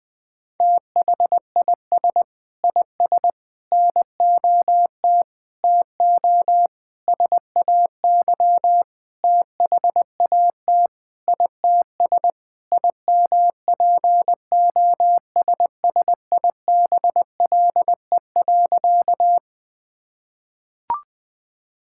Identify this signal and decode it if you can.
Morse code, heard as a keyed tone, about 21 wpm, 700 Hz. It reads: THIS IS NOT TO SAY THAT ITS IMPOSSIBLE.
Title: Sentences with 6-Word Phrases - 20wpm